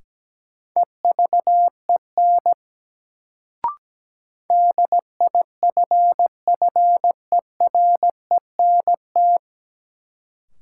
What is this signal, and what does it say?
Morse code, heard as a keyed tone, about 17 wpm, 700 Hz. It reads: EVEN DIFFERENT
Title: Sentences from Top 600 Words - New Word Drill - 17wpm